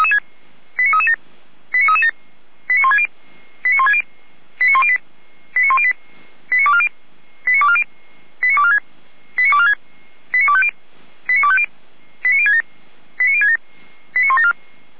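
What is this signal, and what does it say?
Five-tone selective call (ZVEI), heard as quick runs of five tones, tones of about 70 ms. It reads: 89337 89337 89160 89160 89108 89108 89240 89240 89246 89246 89250 89250 89067 89067 89174
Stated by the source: ZVEI five-tone selective call 89075, received on an SDR.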